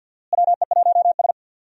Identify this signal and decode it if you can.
Morse code, heard as a keyed tone, about 49 wpm, 700 Hz. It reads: WE1S
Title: USA Call Signs - 4 Characters - Repeated 2x - 50wpm